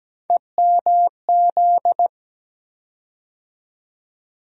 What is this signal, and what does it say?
Morse code, heard as a keyed tone, about 17 wpm, 700 Hz. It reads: EMZ
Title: USA Call Sign Postfixes - 3 Letters 17wpm